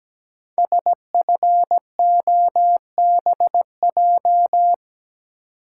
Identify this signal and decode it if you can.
Morse code, heard as a keyed tone, about 17 wpm, 700 Hz. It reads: SFOBJ
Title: Code Groups - 5 Letters - 17wpm